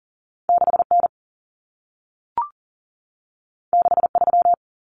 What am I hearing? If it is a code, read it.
Morse code, heard as a keyed tone, about 40 wpm, 700 Hz. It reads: 6N 63
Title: Common Cut Numbers - 2-Digit 40wpm